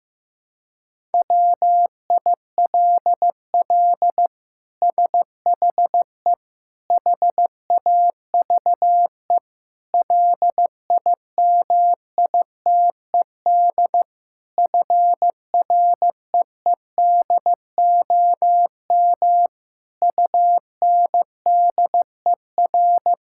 Morse code, heard as a keyed tone, about 16 wpm, 700 Hz. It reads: WILL SHE HAVE LIMITED FREEDOM UNDER
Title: Sentences from Top 1000 Words - Review Part 6 - 15wpm